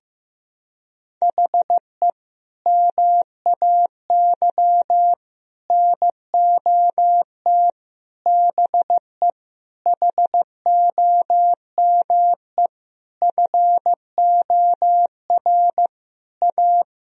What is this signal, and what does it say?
Morse code, heard as a keyed tone, about 15 wpm, 700 Hz. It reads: HE MAY NOT BE HOME FOR A
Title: Sentences from Top 1000 Words - Review Part 2 - 15wpm